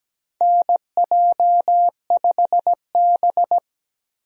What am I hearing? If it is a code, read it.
Morse code, heard as a keyed tone, about 17 wpm, 700 Hz. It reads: NJ5B
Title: USA Call Signs - 4 Characters - 17wpm